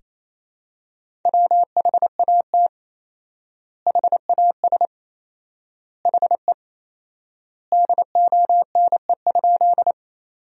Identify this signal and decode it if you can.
Morse code, heard as a keyed tone, about 28 wpm, 700 Hz. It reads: WHAT HAS HE DONE?
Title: 4x Word-Spacing - Sentences from Top 500 Words - 28wpm